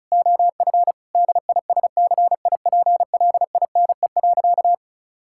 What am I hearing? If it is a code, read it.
Morse code, heard as a keyed tone, about 36 wpm, 700 Hz. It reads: OF DISCIPLINE.
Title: Sentences from Top 1000 Words - Review Part 6 - 35wpm